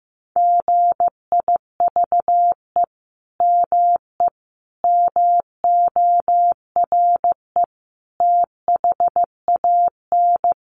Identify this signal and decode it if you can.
Morse code, heard as a keyed tone, about 15 wpm, 700 Hz. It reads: GIVE ME MORE THAN
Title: Sentences from Top 200 Words 15wpm